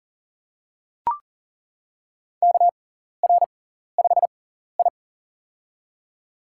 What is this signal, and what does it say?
Morse code, heard as a keyed tone, about 40 wpm, spaced as for 14 wpm, 700 Hz. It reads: KR5I
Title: USA Call Signs - 4 Characters - 6x Character Spacing - 40wpm